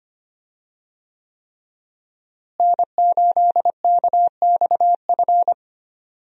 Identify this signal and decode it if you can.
Morse code, heard as a keyed tone, about 25 wpm, 700 Hz. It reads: N8KXF
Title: USA Call Signs - 5 Characters - Repeated 3x - 25wpm